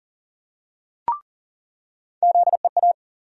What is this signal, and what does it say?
Morse code, heard as a keyed tone, about 37 wpm, 700 Hz. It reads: ZEA
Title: USA Call Sign Postfixes - 3 Letters 40wpm